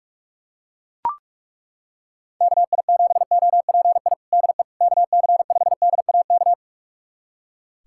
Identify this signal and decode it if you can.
Morse code, heard as a keyed tone, about 41 wpm, 700 Hz. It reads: KI7OPI DE KC5DAK
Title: USA Callsign DE Callsign 45wpm